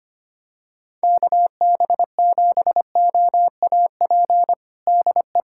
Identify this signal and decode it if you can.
Morse code, heard as a keyed tone, about 25 wpm, 700 Hz. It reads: KB7OAP DE
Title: USA Callsign DE Callsign - 6 Characters - 25wpm